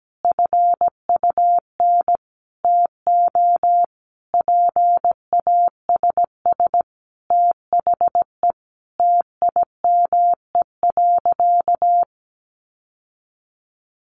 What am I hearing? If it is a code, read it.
Morse code, heard as a keyed tone, about 17 wpm, 700 Hz. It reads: FUN TO PASS THE TIME.
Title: Sentences from Top 700 Words 17wpm